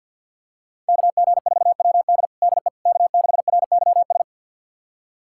Morse code, heard as a keyed tone, about 49 wpm, 700 Hz. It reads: KG4WD DE K6RYS
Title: USA Callsign DE Callsign - 5 Characters - 50wpm